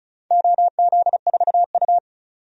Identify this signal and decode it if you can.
Morse code, heard as a keyed tone, about 36 wpm, 700 Hz. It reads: OZ4U
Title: DXCC-100 Call Signs - 4 Characters - 35wpm